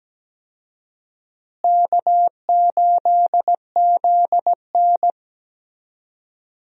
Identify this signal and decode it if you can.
Morse code, heard as a keyed tone, about 17 wpm, 700 Hz. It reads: K8ZN
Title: USA Call Signs - 4 Characters - Repeated 3x - 17wpm